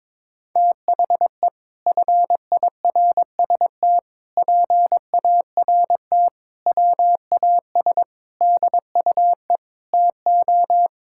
Morse code, heard as a keyed tone, about 22 wpm, 700 Hz. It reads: THE FIRST PART WAS DUE TO
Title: Sentences from Top 1000 Words - Review Part 6 - 22wpm